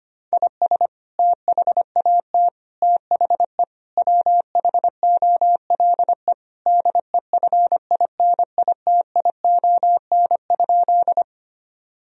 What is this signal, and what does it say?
Morse code, heard as a keyed tone, about 25 wpm, 700 Hz. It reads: IS THAT THE WHOLE DEFINITION?
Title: Sentences from Top 1000 Words - Review Part 4 - 25wpm